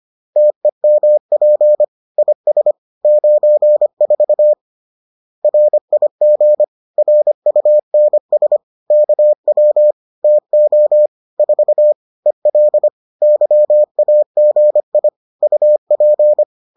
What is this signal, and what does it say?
Morse code, heard as a keyed tone, about 25 wpm, 600 Hz. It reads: TEMP IS 94 RIG RUNS KW TO 4 EL YAGI UP